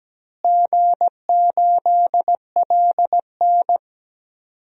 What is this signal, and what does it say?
Morse code, heard as a keyed tone, about 17 wpm, 700 Hz. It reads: G8LN
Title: DXCC-100 Call Signs - 4 Characters - Repeated 2x - 17wpm